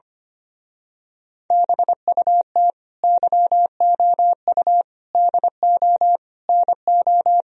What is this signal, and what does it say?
Morse code, heard as a keyed tone, about 25 wpm, 700 Hz. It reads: BUT YOU DO NO
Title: Speed-Racing - Sets of 4 Words 25wpm